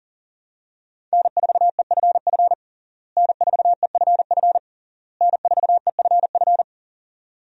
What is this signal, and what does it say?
Morse code, heard as a keyed tone, about 37 wpm, 700 Hz. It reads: N4EFF N4EFF N4EFF
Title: USA Call Signs - 5 Characters - Repeated 3x - 40wpm